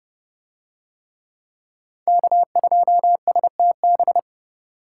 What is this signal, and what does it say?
Morse code, heard as a keyed tone, about 29 wpm, 700 Hz. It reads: K2STB